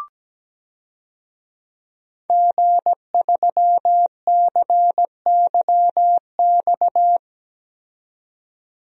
Morse code, heard as a keyed tone, about 17 wpm, 700 Hz. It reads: G3CYX
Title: DXCC-100 Call Signs - 5 Characters - 17wpm